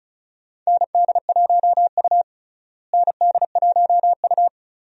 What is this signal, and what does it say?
Morse code, heard as a keyed tone, about 34 wpm, 700 Hz. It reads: ND1U ND1U
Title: USA Call Signs - 4 Characters - Repeated 2x - 35wpm